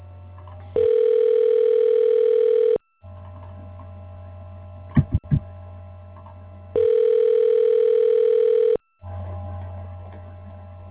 Ringback tone (North America), 440 + 480 Hz.